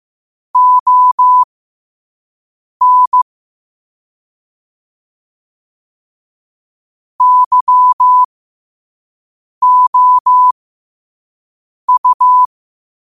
Morse code, heard as a keyed tone, about 15 wpm, spaced as for 5 wpm, 1000 Hz. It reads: ON YOU
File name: qotdfiles / 5wpm